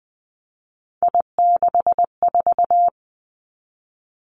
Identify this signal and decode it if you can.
Morse code, heard as a keyed tone, about 20 wpm, 700 Hz. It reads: I64
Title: Three Challenging Letter-Number Combinations 20wpm